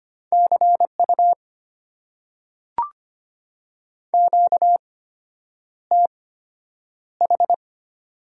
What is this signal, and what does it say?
Morse code, heard as a keyed tone, about 25 wpm, spaced as for 9 wpm, 700 Hz. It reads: CU QTH
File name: Top 90 QSO Elements - 8x Character Spacing - 25wpm